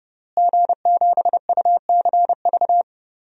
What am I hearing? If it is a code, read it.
Morse code, heard as a keyed tone, about 30 wpm, 700 Hz. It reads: G7UCV
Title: DXCC-100 Call Signs - 5 Characters - Repeated 2x - 30wpm